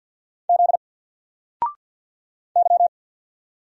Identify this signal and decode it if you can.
Morse code, heard as a keyed tone, about 50 wpm, 700 Hz. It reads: Z Y
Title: Single Letter-Number 50wpm